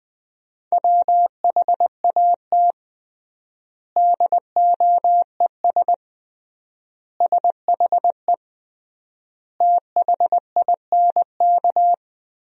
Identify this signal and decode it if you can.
Morse code, heard as a keyed tone, about 21 wpm, 700 Hz. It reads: WHAT DOES SHE THINK
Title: 3x Word-Spacing - Sentences from Top 200 Words - 20wpm